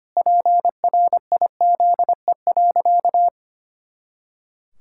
Morse code, heard as a keyed tone, about 25 wpm, 700 Hz. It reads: PRIZE.
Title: Pangram Challenge v2 - 25wpm